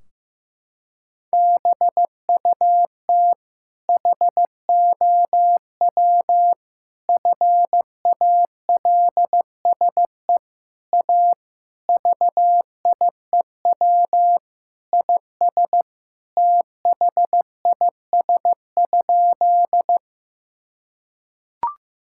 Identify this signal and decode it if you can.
Morse code, heard as a keyed tone, about 16 wpm, 700 Hz. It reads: BUT HOW FALSE A VIEW IS THIS?
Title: Sentences from Top 1600 Words 15wpm